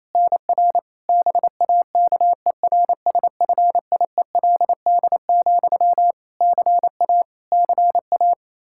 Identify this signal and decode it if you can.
Morse code, heard as a keyed tone, about 28 wpm, 700 Hz. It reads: NR BAKERSFIELD, CA CA
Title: Practice QSO - Exchange 1 - Beginning - 28wpm